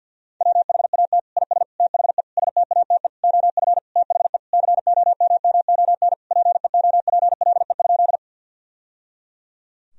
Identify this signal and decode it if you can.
Morse code, heard as a keyed tone, about 50 wpm, 700 Hz. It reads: WHAT IS THE STATE OF THE COMMON PEOPLE?
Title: Sentences from Top 1000 Words - Review Part 3 - 50wpm